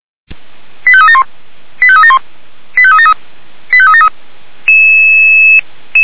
Five-tone selective call (ZVEI), heard as quick runs of five tones, tones of about 70 ms. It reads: 85371 85371 85373 85373
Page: /464/SDR/ZVEI/